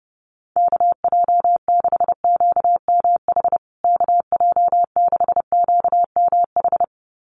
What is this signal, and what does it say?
Morse code, heard as a keyed tone, about 30 wpm, 700 Hz. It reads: KJ6QMH KJ6QMH
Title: USA Call Signs Repeated 2x 30wpm